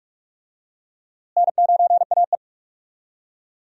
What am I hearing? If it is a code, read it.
Morse code, heard as a keyed tone, about 46 wpm, 700 Hz. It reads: N9AE
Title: USA Call Signs - 4 Characters - Repeated 2x - 45wpm